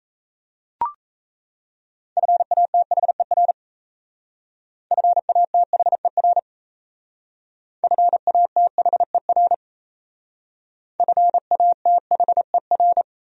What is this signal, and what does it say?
Morse code, heard as a keyed tone, about 35 wpm, 700 Hz. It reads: FATHER FATHER FATHER FATHER
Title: Speed-Racing Top 500 Words 28wpm